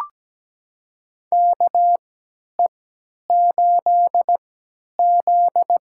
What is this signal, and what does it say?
Morse code, heard as a keyed tone, about 17 wpm, spaced as for 10 wpm, 700 Hz. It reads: KE8Z
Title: USA Call Signs - 4 Characters - 3x Character Spacing - 17wpm